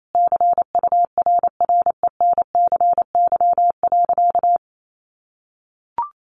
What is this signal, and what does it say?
Morse code, heard as a keyed tone, about 28 wpm, 700 Hz. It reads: CURRENCY.
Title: Sentences from Top 1500 Words 28wpm